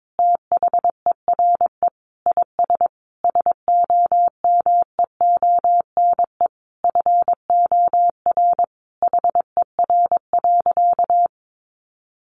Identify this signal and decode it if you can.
Morse code, heard as a keyed tone, about 22 wpm, 700 Hz. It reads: THERE IS SOMEONE FOR HER.
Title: Sentences from Top 600 Words 22wpm